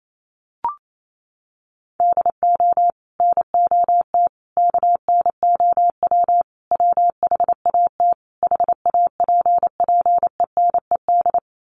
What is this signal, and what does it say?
Morse code, heard as a keyed tone, about 28 wpm, 700 Hz. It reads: DO NOT KNOW WHAT HAPPENED
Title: Sets of 5 Words Encore2 28wpm